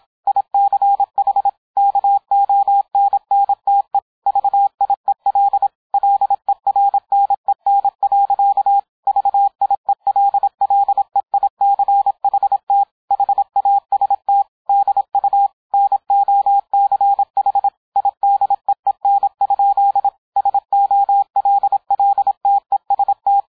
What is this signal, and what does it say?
Morse code, heard as a keyed tone, about 27 wpm, 800 Hz, with radio noise. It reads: ICH KONNTE VIEL LERNEN. VIELLEICHT HAST DU NOCH IDEEN? SOLLTEST